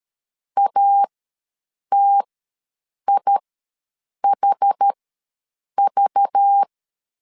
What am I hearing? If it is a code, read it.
Morse code, heard as a keyed tone, about 13 wpm, spaced as for 7 wpm, 800 Hz. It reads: ATIHV
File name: LEZIONI / 5 LBVC / 5 LBV ws 12wpm 3015